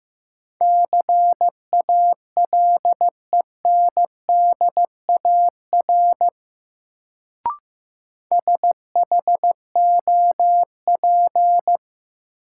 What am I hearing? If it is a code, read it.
Morse code, heard as a keyed tone, about 15 wpm, 700 Hz. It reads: CALENDAR SHOP
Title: Sentences from Top 1600 Words - New Word Drill Alt - 15wpm